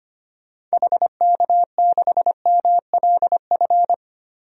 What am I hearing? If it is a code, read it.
Morse code, heard as a keyed tone, about 25 wpm, 700 Hz. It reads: HK6MLF
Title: DXCC-100 Call Signs - 6 Characters - Repeated 2x - 25wpm